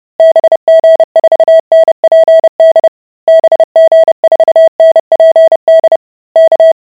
Morse code, heard as a keyed tone, about 30 wpm, 650 Hz. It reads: BG4NPD BG4NPD K